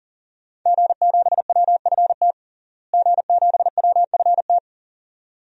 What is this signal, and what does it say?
Morse code, heard as a keyed tone, about 40 wpm, 700 Hz. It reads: G7WFT G7WFT
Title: DXCC-100 Call Signs - 5 Characters - Repeated 2x - 40wpm